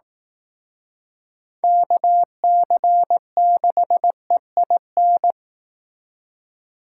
Morse code, heard as a keyed tone, about 18 wpm, 700 Hz. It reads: KC6EIN
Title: Speed Racing - USA 2x3 Call Signs 15wpm